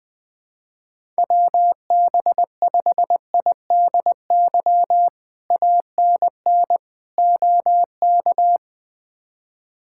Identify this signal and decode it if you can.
Morse code, heard as a keyed tone, about 20 wpm, 700 Hz. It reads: WB5IDY ANN OK
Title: K1USN SST QSO Practice - 20wpm